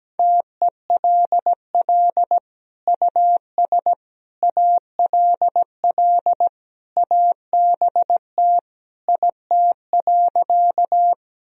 Morse code, heard as a keyed tone, about 17 wpm, 700 Hz. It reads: TELL US ALL ABT IT.
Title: Sentences from Top 200 Words and Abbreviations 17wpm